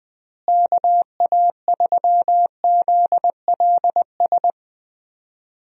Morse code, heard as a keyed tone, about 20 wpm, 700 Hz. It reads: KA3ZLS
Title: USA Call Signs - 6 Characters - Repeated 3x - 20wpm